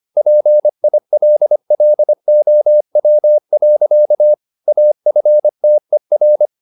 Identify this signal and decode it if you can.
Morse code, heard as a keyed tone, about 25 wpm, 600 Hz. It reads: PILLOW. AFTER